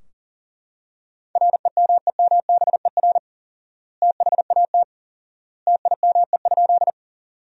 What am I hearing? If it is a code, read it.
Morse code, heard as a keyed tone, about 40 wpm, 700 Hz. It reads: REMEMBER THAT TIME?
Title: 4x Word-Spacing - Sentences from Top 400 Words - 40wpm